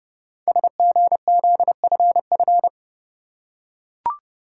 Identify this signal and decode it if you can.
Morse code, heard as a keyed tone, about 30 wpm, 690 Hz. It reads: SGZFF
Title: Code Groups - 5 Letters - 30wpm